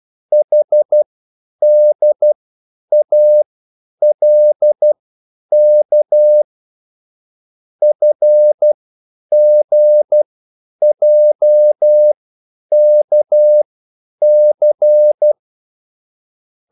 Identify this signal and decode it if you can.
Morse code, heard as a keyed tone, about 12 wpm, spaced as for 9 wpm, 600 Hz. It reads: HDALK FGJKC